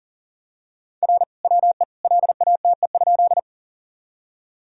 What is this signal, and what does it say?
Morse code, heard as a keyed tone, about 40 wpm, 700 Hz. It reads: R WE LATE?
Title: Sentences from Top 300 Words and Abbreviations 40wpm